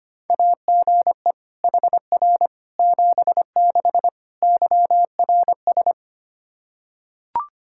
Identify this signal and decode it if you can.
Morse code, heard as a keyed tone, about 25 wpm, 700 Hz. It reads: AGE HR 76 YRS
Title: Practice QSO - Exchange 3 - More Info - 25wpm